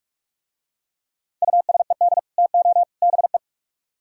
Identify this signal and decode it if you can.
Morse code, heard as a keyed tone, about 44 wpm, 700 Hz. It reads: USED TO BE